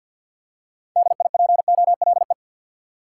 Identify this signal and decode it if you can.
Morse code, heard as a keyed tone, about 53 wpm, 700 Hz. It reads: DIPOLE
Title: Top 180 QSO Elements 50wpm